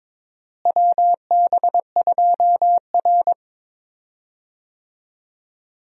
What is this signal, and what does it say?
Morse code, heard as a keyed tone, about 22 wpm, 700 Hz. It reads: WB2R
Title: USA Call Signs - 4 Characters - Repeated 3x - 22wpm